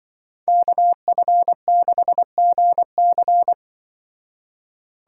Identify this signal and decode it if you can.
Morse code, heard as a keyed tone, about 24 wpm, 700 Hz. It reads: KF6GC
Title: Speed Racing - USA 2x2 Call Signs 20wpm